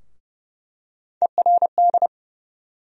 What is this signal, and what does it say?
Morse code, heard as a keyed tone, about 30 wpm, 700 Hz. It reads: ERD
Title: USA Call Sign Postfixes - 3 Letters 30wpm